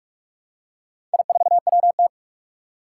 Morse code, heard as a keyed tone, about 44 wpm, 700 Hz. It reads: I4WT
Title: Four Challenging Letter-Number Combinations 45wpm